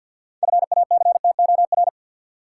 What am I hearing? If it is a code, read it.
Morse code, heard as a keyed tone, about 53 wpm, 700 Hz. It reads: FACTOR
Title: Sentences from Top 1300 Words - New Word Drill Alt - 50wpm